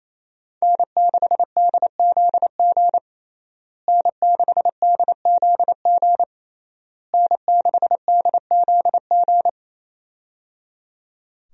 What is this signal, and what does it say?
Morse code, heard as a keyed tone, about 28 wpm, 700 Hz. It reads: N6DZG N6DZG N6DZG
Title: USA Call Signs - 5 Characters - Repeated 3x - 28wpm